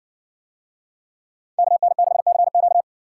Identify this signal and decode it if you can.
Morse code, heard as a keyed tone, about 58 wpm, 700 Hz. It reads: KN6CQ